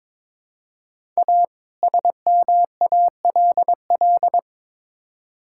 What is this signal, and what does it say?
Morse code, heard as a keyed tone, about 22 wpm, 700 Hz. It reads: A SMALL
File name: Build A Phrase - 2-Word Phrases - 22wpm